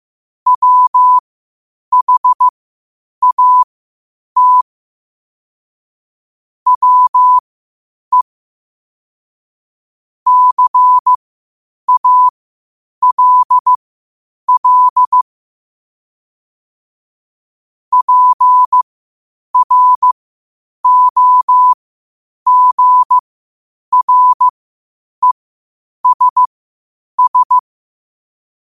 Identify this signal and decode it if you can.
Morse code, heard as a keyed tone, about 15 wpm, spaced as for 8 wpm, 1000 Hz. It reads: WHAT WE CALL PROGRESS
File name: qotdfiles / 8wpm